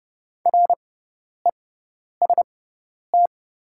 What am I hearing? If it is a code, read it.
Morse code, heard as a keyed tone, about 30 wpm, spaced as for 10 wpm, 700 Hz. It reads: REST
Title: 6x Character Spacing - Top 500 Words - 30wpm